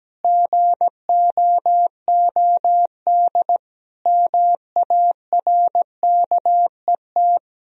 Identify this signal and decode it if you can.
Morse code, heard as a keyed tone, about 17 wpm, 700 Hz. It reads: GOOD MARKET
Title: Sentences from Top 1000 Words - Review Part 3 - 17wpm